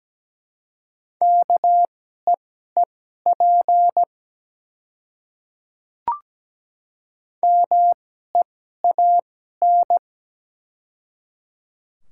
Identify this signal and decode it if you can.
Morse code, heard as a keyed tone, about 17 wpm, spaced as for 12 wpm, 700 Hz. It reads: KEEP MEAN
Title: 2x Character Spacing - Top 300 Words - 17wpm